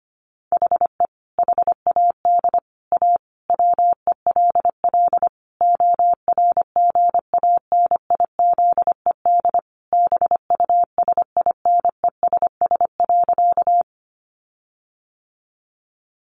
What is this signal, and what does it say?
Morse code, heard as a keyed tone, about 25 wpm, 700 Hz. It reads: HE HAD A WELL ORGANIZED BUSINESS.
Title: Sentences from Top 1300 Words 25wpm